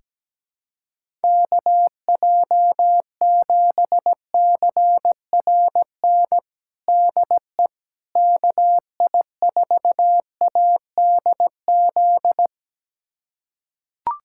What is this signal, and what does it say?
Morse code, heard as a keyed tone, about 17 wpm, 700 Hz. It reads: KJ7CRN DE KI4ADZ
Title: USA Callsign DE Callsign - 6 Characters - 17wpm